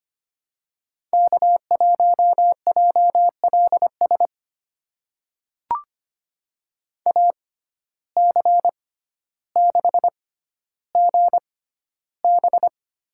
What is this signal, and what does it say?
Morse code, heard as a keyed tone, about 25 wpm, spaced as for 13 wpm, 700 Hz. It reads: K1JLS AC6GB
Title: USA Call Signs - 5 Characters - 6x Character Spacing - 25wpm